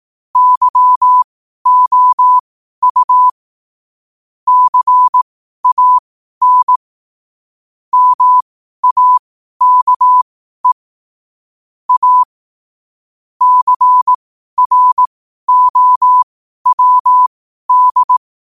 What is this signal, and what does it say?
Morse code, heard as a keyed tone, about 18 wpm, spaced as for 12 wpm, 1000 Hz. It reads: YOU CAN MAKE A CROWD
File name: qotdfiles / 12wpm